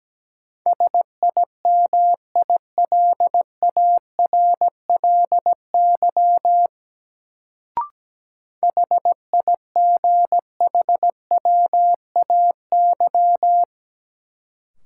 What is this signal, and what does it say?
Morse code, heard as a keyed tone, about 17 wpm, 700 Hz. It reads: SIMILARLY HIGHWAY
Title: Sentences from Top 1400 Words - New Word Drill - 17wpm